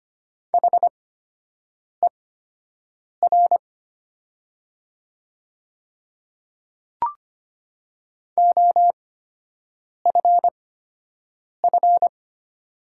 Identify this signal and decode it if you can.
Morse code, heard as a keyed tone, about 25 wpm, spaced as for 7 wpm, 700 Hz. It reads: HER OFF
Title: 8x Character Spacing - Top 500 Words - 25wpm